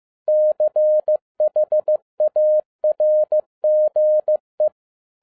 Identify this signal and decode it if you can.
Morse code, heard as a keyed tone, about 15 wpm, 610 Hz. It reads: CHARGE
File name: 2011-07-mp3 / n12 T8XO 15wpm